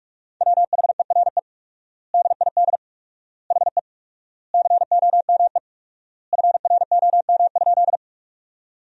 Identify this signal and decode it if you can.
Morse code, heard as a keyed tone, about 46 wpm, 700 Hz. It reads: WHERE DID HE COME FROM?
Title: 4x Word-Spacing - Sentences from Top 200 Words - 45wpm